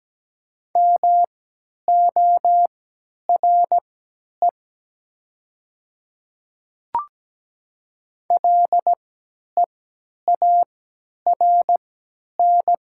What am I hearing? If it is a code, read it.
Morse code, heard as a keyed tone, about 17 wpm, spaced as for 10 wpm, 700 Hz. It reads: MORE LEARN
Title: 3x Character Spacing - Top 300 Words - 17wpm